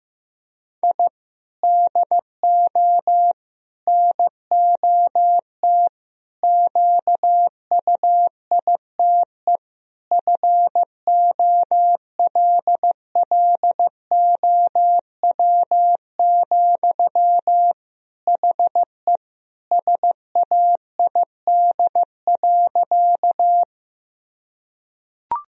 Morse code, heard as a keyed tone, about 15 wpm, 700 Hz. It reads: I DO NOT QUITE FOLLOW, HE SAID.
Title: Sentences from Top 1000 Words - Review Part 1 - 15wpm